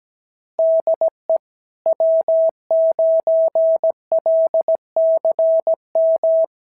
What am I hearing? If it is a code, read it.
Morse code, heard as a keyed tone, about 17 wpm, 650 Hz. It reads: DE W9LCM
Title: SOTA QSO Practice 17wpm